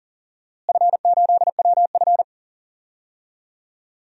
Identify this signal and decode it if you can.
Morse code, heard as a keyed tone, about 40 wpm, 700 Hz. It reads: F8WF
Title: DXCC-100 Call Signs - 4 Characters - Repeated 2x - 40wpm